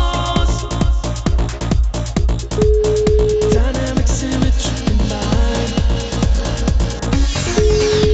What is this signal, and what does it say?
Ringback tone (Europe, 425 Hz), with noise.